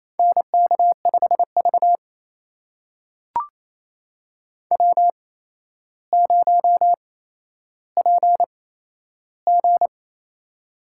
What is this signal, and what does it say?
Morse code, heard as a keyed tone, about 28 wpm, spaced as for 12 wpm, 700 Hz. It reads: NK5V W0PG
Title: USA Call Signs - 4 Characters - 8x Character Spacing - 28wpm